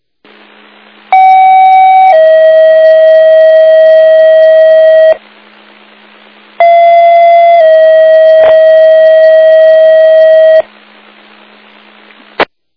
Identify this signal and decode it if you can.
Two-tone page, heard as a one-second tone, then a three-second tone, about 760 then 640 Hz, about 680 then 640 Hz. It is heard from 1.1 to 10.6 s.